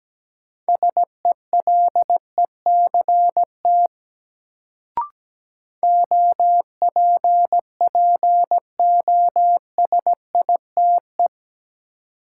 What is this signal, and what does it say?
Morse code, heard as a keyed tone, about 17 wpm, 700 Hz. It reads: SELECT OPPOSITE
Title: Sentences from Top 1300 Words - New Word Drill Alt - 17wpm